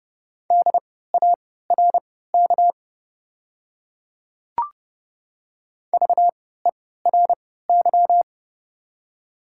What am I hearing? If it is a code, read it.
Morse code, heard as a keyed tone, about 30 wpm, spaced as for 17 wpm, 690 Hz. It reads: DARK VERY
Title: 3x Character Spacing - Top 500 Words - 30wpm